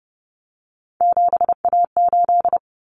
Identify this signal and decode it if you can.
Morse code, heard as a keyed tone, about 30 wpm, 700 Hz. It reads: 7A8
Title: Common Cut Numbers - 3-Digit 30wpm